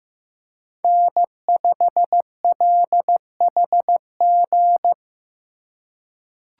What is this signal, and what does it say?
Morse code, heard as a keyed tone, about 16 wpm, 700 Hz. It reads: N5LHG
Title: USA Call Signs - 5 Characters - 3x Character Spacing - 15wpm